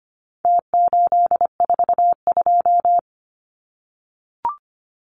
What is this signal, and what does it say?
Morse code, heard as a keyed tone, about 25 wpm, 700 Hz. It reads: T842
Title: Numbers - Contest Style - 4-Digit 25wpm